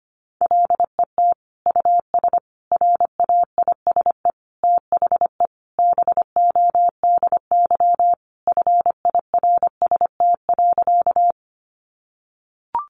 Morse code, heard as a keyed tone, about 25 wpm, 700 Hz. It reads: LET US RAISE THE BODY FIRST.
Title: Sentences from Top 1400 Words 25wpm